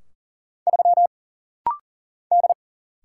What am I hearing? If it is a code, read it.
Morse code, heard as a keyed tone, about 40 wpm, 700 Hz. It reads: 3 D
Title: Single Letter-Number Alt - Rapid-Fire 40wpm